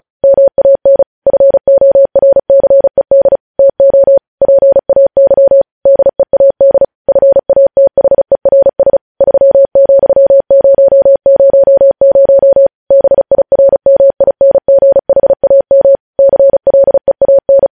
Morse code, heard as a keyed tone, about 36 wpm, 550 Hz. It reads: MAN FORCED TO PAY DEAD FATHERS 3,000 BIRMINGHAM CLEAN
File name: podcast / News headlines-1x-35x35